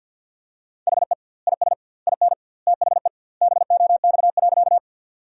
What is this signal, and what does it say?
Morse code, heard as a keyed tone, about 52 wpm, 700 Hz. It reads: HE IS IN THE BOX.
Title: 2x Word-Spacing - Sentences from Top 500 Words - 50wpm